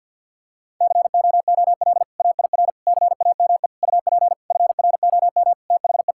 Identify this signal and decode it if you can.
Morse code, heard as a keyed tone, about 50 wpm, 700 Hz. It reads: COOL AIR CAME UP FROM THE
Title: Sentences from Top 1600 Words 50wpm